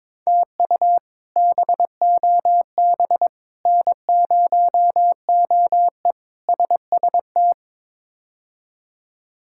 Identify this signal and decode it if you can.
Morse code, heard as a keyed tone, about 22 wpm, 700 Hz. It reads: TU BOB N0OE SST
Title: K1USN SST QSO Practice - 22wpm